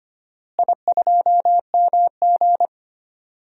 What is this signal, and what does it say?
Morse code, heard as a keyed tone, about 25 wpm, 700 Hz. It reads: I2MG
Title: DXCC-100 Call Signs - 4 Characters - Repeated 2x - 25wpm